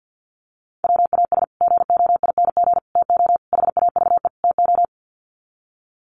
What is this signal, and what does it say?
Morse code, heard as a keyed tone, about 50 wpm, 700 Hz. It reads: WAS GOING TO HAVE TO
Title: Sets of 5 Words Encore 50wpm